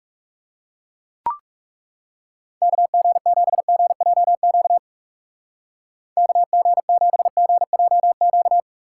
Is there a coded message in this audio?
Yes: Morse code from 2.6 to 8.6 s, heard as a keyed tone, about 40 wpm, 700 Hz. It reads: KG7GJQ KG7GJQ